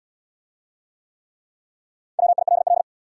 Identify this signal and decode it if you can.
Morse code, heard as a keyed tone, about 66 wpm, 700 Hz. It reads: WILL